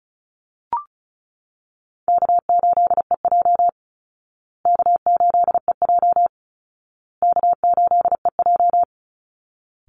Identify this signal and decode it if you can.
Morse code, heard as a keyed tone, about 34 wpm, 700 Hz. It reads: K8EJ K8EJ K8EJ
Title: Easy Call Signs Repeated 3x - 35wpm